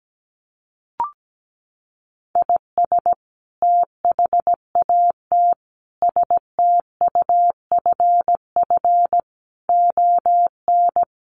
Morse code, heard as a keyed tone, about 17 wpm, 700 Hz. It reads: IS THAT STUFF ON
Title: Sentences from Top 1400 Words 17wpm